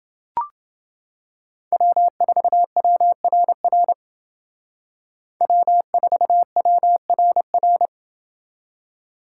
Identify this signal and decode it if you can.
Morse code, heard as a keyed tone, about 28 wpm, 700 Hz. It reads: W4WRR W4WRR